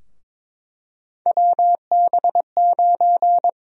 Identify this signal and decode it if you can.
Morse code, heard as a keyed tone, about 22 wpm, 700 Hz. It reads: WB9
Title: USA Call Sign Prefixes - 22wpm